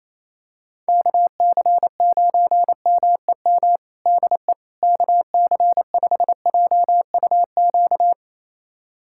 Morse code, heard as a keyed tone, about 28 wpm, 700 Hz. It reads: KC9MEM DE KC5JUQ
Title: USA Callsign DE Callsign - 6 Characters - 28wpm